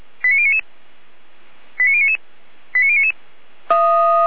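Five-tone selective call (ZVEI), heard as quick runs of five tones, tones of about 70 ms. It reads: 89099 89099 89099
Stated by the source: ZVEI five-tone selective call received on SDR.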